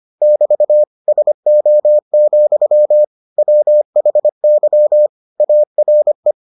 Morse code, heard as a keyed tone, about 25 wpm, 600 Hz. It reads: <BT> SO, WHY ARE